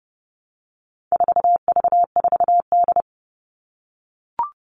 Morse code, heard as a keyed tone, about 30 wpm, 700 Hz. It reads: 4V4D
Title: All Cut Numbers - 4-Digit 30wpm